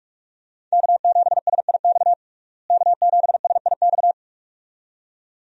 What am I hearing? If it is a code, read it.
Morse code, heard as a keyed tone, about 47 wpm, 700 Hz. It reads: K7SIX K7SIX
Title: USA Call Signs - 5 Characters - Repeated 2x - 45wpm